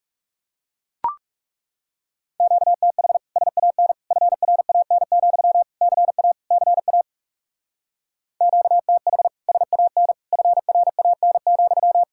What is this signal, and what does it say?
Morse code, heard as a keyed tone, about 39 wpm, 700 Hz. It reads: QTH SAN FRAN, CA CA QTH SAN FRAN,